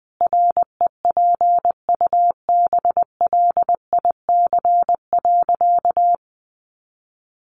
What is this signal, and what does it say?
Morse code, heard as a keyed tone, about 20 wpm, 700 Hz. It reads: REPUBLIC.
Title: Sentences from Top 1500 Words 20wpm